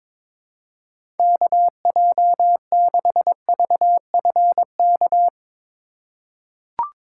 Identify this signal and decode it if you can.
Morse code, heard as a keyed tone, about 22 wpm, 700 Hz. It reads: KJ6VFK
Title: USA Call Signs - 6 Characters - 2x Character Spacing - 22wpm